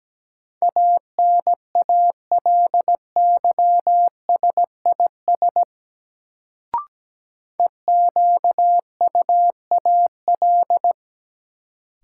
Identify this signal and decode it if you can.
Morse code, heard as a keyed tone, about 17 wpm, 700 Hz. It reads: ANALYSIS EQUAL
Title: Sentences from Top 1400 Words - New Word Drill Alt - 17wpm